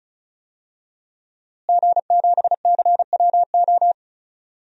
Morse code, heard as a keyed tone, about 36 wpm, 700 Hz. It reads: G7CWO
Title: DXCC-100 Call Signs - 5 Characters - Repeated 2x - 35wpm